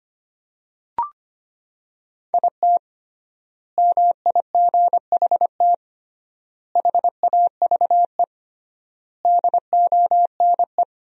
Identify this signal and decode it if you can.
Morse code, heard as a keyed tone, about 25 wpm, 700 Hz. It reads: IT MIGHT HAVE DONE
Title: 3x Word-Spacing - Sentences from Top 500 Words - 25wpm